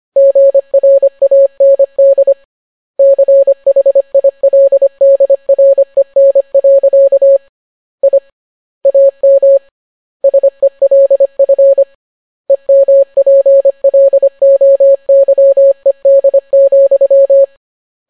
Morse code, heard as a keyed tone, about 25 wpm, 550 Hz. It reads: GRAND CHILDREN. I AM SELF EMPLOYED,